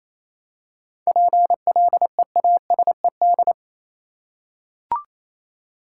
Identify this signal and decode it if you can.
Morse code, heard as a keyed tone, about 28 wpm, 700 Hz. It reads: PLEASED
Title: Seven Letter Words - 28wpm